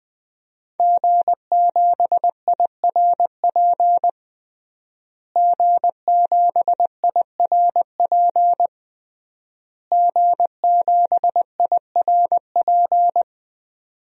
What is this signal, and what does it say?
Morse code, heard as a keyed tone, about 20 wpm, 700 Hz. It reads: G7IRP G7IRP G7IRP
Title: DXCC-100 Call Signs - 5 Characters - Repeated 3x - 20wpm